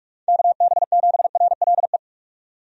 Morse code, heard as a keyed tone, about 43 wpm, 700 Hz. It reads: KB7RLE
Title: USA Call Signs - 6 Characters - 45wpm